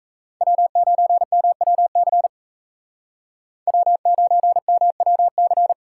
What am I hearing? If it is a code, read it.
Morse code, heard as a keyed tone, about 36 wpm, 700 Hz. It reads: W9MWC W9MWC